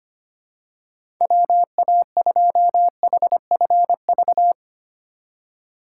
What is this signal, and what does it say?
Morse code, heard as a keyed tone, about 25 wpm, 700 Hz. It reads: WA2HFV
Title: USA Call Signs - 6 Characters - 3x Character Spacing - 25wpm